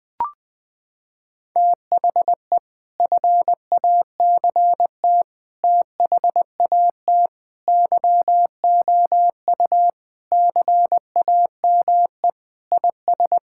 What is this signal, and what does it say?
Morse code, heard as a keyed tone, about 21 wpm, 700 Hz. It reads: THE FACT THAT YOU CAME IS
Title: Sentences with 3-Word Phrases - 20wpm